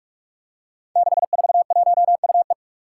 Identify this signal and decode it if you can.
Morse code, heard as a keyed tone, about 42 wpm, 700 Hz. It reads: 641UE
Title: Five Challenging Letter-Number Combinations 45wpm